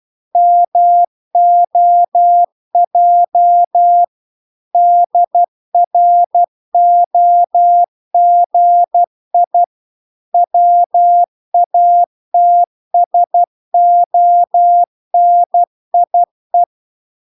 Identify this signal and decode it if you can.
Morse code, heard as a keyed tone, about 12 wpm, 700 Hz. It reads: MOJ DROGI WATSONIE